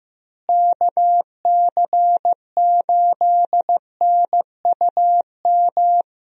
Morse code, heard as a keyed tone, about 15 wpm, 700 Hz. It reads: KC8NUM
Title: USA Call Signs - 6 Characters - Repeated 3x - 15wpm